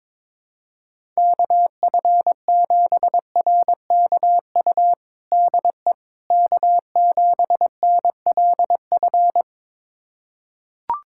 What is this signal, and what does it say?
Morse code, heard as a keyed tone, about 22 wpm, 700 Hz. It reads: KF7RKU DE K7NLF
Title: USA Callsign DE Callsign 22wpm